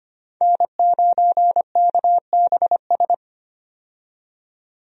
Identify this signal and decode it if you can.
Morse code, heard as a keyed tone, about 25 wpm, 700 Hz. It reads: N9KBS
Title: USA Call Signs - 5 Characters - 25wpm